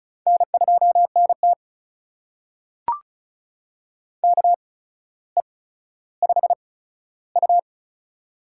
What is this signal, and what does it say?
Morse code, heard as a keyed tone, about 36 wpm, spaced as for 15 wpm, 700 Hz. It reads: N2NT KE5U
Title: USA Call Signs - 4 Characters - 8x Character Spacing - 35wpm